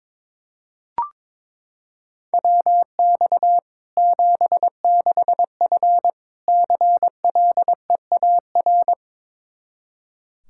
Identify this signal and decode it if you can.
Morse code, heard as a keyed tone, about 22 wpm, 700 Hz. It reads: WX 76F CLEAR
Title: K1USN SST QSO Practice 22wpm